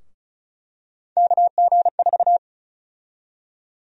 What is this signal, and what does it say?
Morse code, heard as a keyed tone, about 36 wpm, 700 Hz. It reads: KG4
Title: USA Call Sign Prefixes - 35wpm